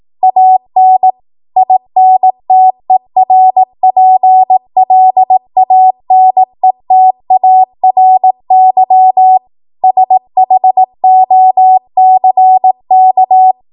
Morse code, heard as a keyed tone, about 18 wpm, 750 Hz. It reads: AN INTERPLANETARY SHOCK
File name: Morse / 18WPM